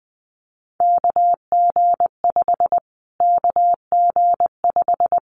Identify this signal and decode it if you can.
Morse code, heard as a keyed tone, about 20 wpm, 700 Hz. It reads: KG5 KG5
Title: Callsign Prefixes for Top 25 DXCC - Repeated 3x - 20wpm